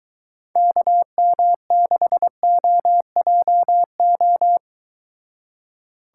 Morse code, heard as a keyed tone, about 22 wpm, 700 Hz. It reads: KM6OJO